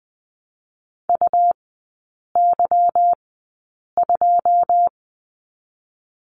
Morse code, heard as a keyed tone, about 20 wpm, spaced as for 8 wpm, 700 Hz. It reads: UY2
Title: 166 Lesson 27e - Number 7 - Three Characters 20wpm